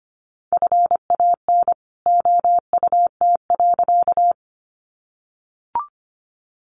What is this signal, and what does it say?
Morse code, heard as a keyed tone, about 25 wpm, 700 Hz. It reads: FAN OUT.
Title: Sentences from Top 1300 Words 25wpm